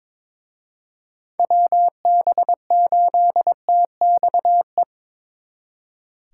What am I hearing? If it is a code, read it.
Morse code, heard as a keyed tone, about 25 wpm, 700 Hz. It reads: WB8TXE
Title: Speed Racing - USA 2x3 Call Signs 22wpm